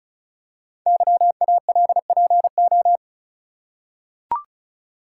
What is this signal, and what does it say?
Morse code, heard as a keyed tone, about 36 wpm, 690 Hz. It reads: YALPO
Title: Code Groups - 5 Letters - 35wpm